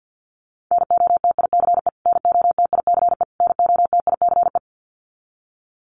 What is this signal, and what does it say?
Morse code, heard as a keyed tone, about 50 wpm, 700 Hz. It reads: NOTICE NOTICE NOTICE
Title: Top 500 Words Repeated 50wpm